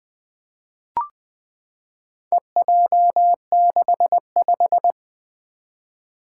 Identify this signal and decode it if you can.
Morse code, heard as a keyed tone, about 20 wpm, 700 Hz. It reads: EJ65
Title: Four Challenging Letter-Number Combinations 20wpm